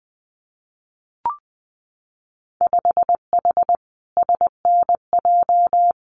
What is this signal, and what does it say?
Morse code, heard as a keyed tone, about 19 wpm, 700 Hz. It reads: 5H SNJ
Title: Winter Field Day Exchanges Encore 20wpm